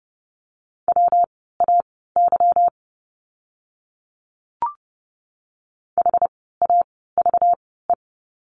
Morse code, heard as a keyed tone, about 30 wpm, spaced as for 17 wpm, 700 Hz. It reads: WAY HAVE
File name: course / 30:10wpm / 230 Lesson 37f - Letter x - Words 30wpm